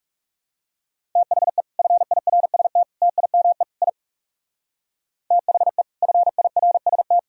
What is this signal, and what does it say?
Morse code, heard as a keyed tone, about 35 wpm, 700 Hz. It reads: THE FIRST TIME I THE FIRST
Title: Speed-Racing - Sets of 4 Words 30wpm